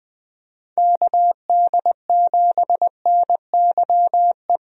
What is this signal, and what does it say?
Morse code, heard as a keyed tone, about 20 wpm, 700 Hz. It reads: KD7NYE
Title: USA Call Signs - 6 Characters - 20wpm